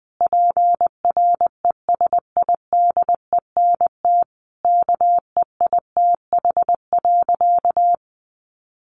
Morse code, heard as a keyed tone, about 20 wpm, 700 Hz. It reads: PRESIDENT KEITH.
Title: A Sound Of Thunder 20wpm